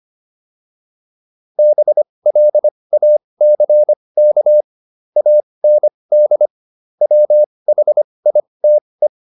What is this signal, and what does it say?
Morse code, heard as a keyed tone, about 25 wpm, spaced as for 20 wpm, 600 Hz. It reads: BLACK AND WHITE